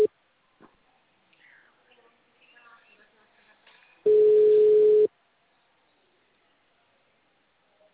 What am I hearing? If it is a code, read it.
Ringback tone (Europe, 425 Hz).